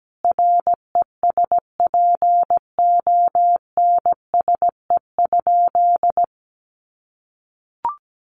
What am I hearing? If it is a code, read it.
Morse code, heard as a keyed tone, about 17 wpm, 700 Hz. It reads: RESPONSE?
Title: Sentences from Top 1400 Words 17wpm